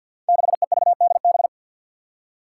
Morse code, heard as a keyed tone, about 50 wpm, 700 Hz. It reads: 6EVDB
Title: Five Challenging Letter-Number Combinations 50wpm